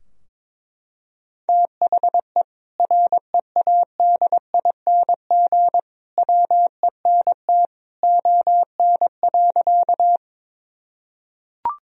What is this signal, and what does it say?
Morse code, heard as a keyed tone, about 22 wpm, 700 Hz. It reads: THE READING WENT ON.
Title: Sentences from Top 1000 Words - Review Part 3 - 22wpm